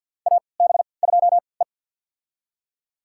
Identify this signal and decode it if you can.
Morse code, heard as a keyed tone, about 52 wpm, spaced as for 29 wpm, 700 Hz. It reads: AB2E
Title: USA Call Signs - 4 Characters - 3x Character Spacing - 50wpm